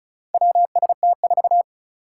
Morse code, heard as a keyed tone, about 36 wpm, 700 Hz. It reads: WST4
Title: Four Challenging Letter-Number Combinations 35wpm